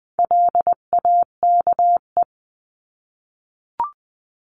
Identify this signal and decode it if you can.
Morse code, heard as a keyed tone, about 20 wpm, 700 Hz. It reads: LAKE
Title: 259 Lesson 42d - Words 18-20wpm